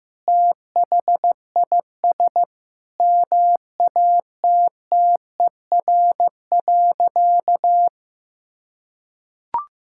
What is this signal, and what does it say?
Morse code, heard as a keyed tone, about 15 wpm, 700 Hz. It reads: THIS MATTER.
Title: Sentences from Top 1600 Words 15wpm